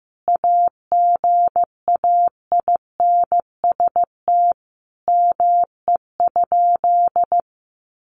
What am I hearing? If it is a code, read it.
Morse code, heard as a keyed tone, about 15 wpm, 700 Hz. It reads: AGAINST ME?
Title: Sentences from Top 1000 Words 15wpm